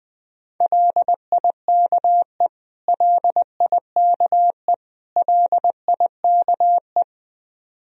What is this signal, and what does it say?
Morse code, heard as a keyed tone, about 20 wpm, 700 Hz. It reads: LIKE LIKE LIKE
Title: Top 100 Words Repeated 20wpm